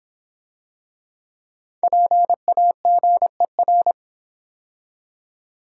Morse code, heard as a keyed tone, about 26 wpm, 700 Hz. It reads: PAGER